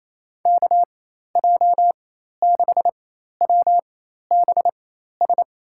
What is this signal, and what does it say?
Morse code, heard as a keyed tone, about 28 wpm, spaced as for 13 wpm, 700 Hz. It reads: KJ6WBS
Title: USA Call Signs - 4x Character Spacing - 28wpm